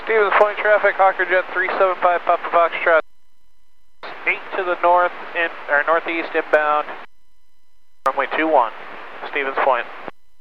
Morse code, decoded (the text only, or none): none